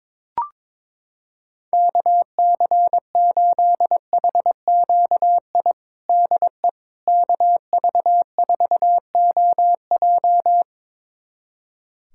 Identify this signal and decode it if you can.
Morse code, heard as a keyed tone, about 22 wpm, 700 Hz. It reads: KC8HQI DE KV4OJ
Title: USA Callsign DE Callsign 22wpm